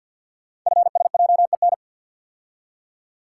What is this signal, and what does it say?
Morse code, heard as a keyed tone, about 50 wpm, 690 Hz. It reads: FSJEN
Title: Code Groups - 5 Letters - 50wpm